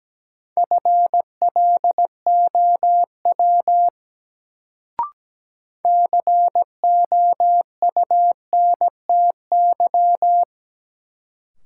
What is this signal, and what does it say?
Morse code, heard as a keyed tone, about 17 wpm, 700 Hz. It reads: FLOW COUNTY
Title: Sentences from Top 1200 Words - New Word Drill Alt - 17wpm